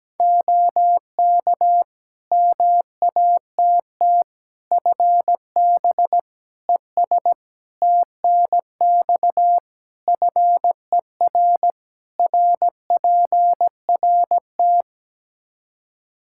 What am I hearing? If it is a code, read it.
Morse code, heard as a keyed tone, about 17 wpm, 700 Hz. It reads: OK MATT FB ES TNX FER RPRT
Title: Practice QSO - Exchange 2 - Info - 17wpm